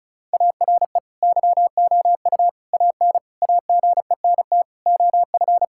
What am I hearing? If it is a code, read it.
Morse code, heard as a keyed tone, about 36 wpm, 700 Hz. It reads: ARE YOU AN AGENT OF
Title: Sentences from Top 1600 Words 35wpm